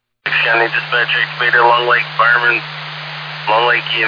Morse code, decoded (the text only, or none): none